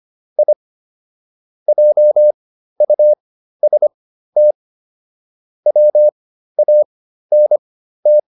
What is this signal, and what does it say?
Morse code, heard as a keyed tone, about 25 wpm, spaced as for 13 wpm, 600 Hz. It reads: I JUST WANT